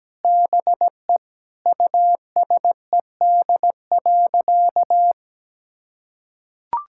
Morse code, heard as a keyed tone, about 17 wpm, 700 Hz. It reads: BE USED.
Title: Sentences from Top 1600 Words 17wpm